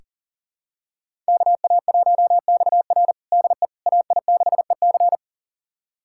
Morse code, heard as a keyed tone, about 37 wpm, 700 Hz. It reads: KA1XR DE AI6EC
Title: USA Callsign DE Callsign - 5 Characters - 40wpm